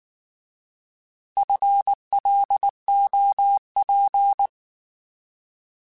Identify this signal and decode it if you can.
Morse code, heard as a keyed tone, about 19 wpm, 800 Hz. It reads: FLOP
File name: Alphabetical / Alphabetical mp3s / Fs @18